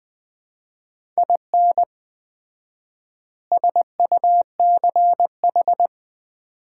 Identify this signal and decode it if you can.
Morse code, heard as a keyed tone, about 20 wpm, 700 Hz. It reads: IN SUCH
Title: 4x Word-Spacing - Sets of 5 Words - 20wpm